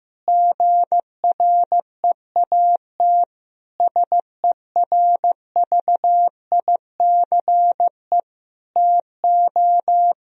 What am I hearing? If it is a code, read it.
Morse code, heard as a keyed tone, about 16 wpm, 700 Hz. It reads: GREAT SERVICE TO
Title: Sentences from Top 1000 Words - Review Part 4 - 15wpm